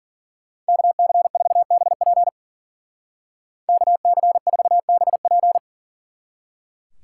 Morse code, heard as a keyed tone, about 47 wpm, 700 Hz. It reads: KC4BP KC4BP
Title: Speed Racing - USA 2x2 Call Signs 40wpm